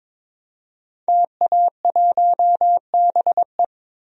Morse code, heard as a keyed tone, about 21 wpm, 700 Hz. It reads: TA1BE
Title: DXCC-100 Call Signs - 5 Characters - Repeated 3x - 22wpm